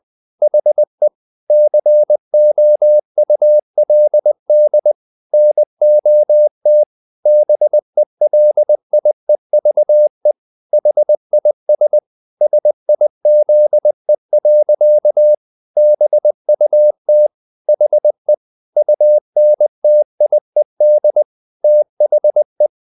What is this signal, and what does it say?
Morse code, heard as a keyed tone, about 20 wpm, 600 Hz. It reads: HE COULD NOT BELIEVE HIS SIZE. BUT HE UNTIED THE
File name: the old man and the sea part 2 20wpm